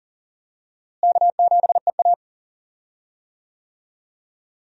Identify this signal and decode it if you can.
Morse code, heard as a keyed tone, about 40 wpm, 700 Hz. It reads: K7EA
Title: USA Call Signs - 4 Characters - Repeated 3x - 40wpm